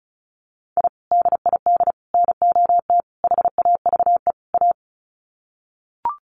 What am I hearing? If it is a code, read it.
Morse code, heard as a keyed tone, about 34 wpm, 700 Hz. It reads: I DID NOT HAVE A
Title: Sets of 5 Words 35wpm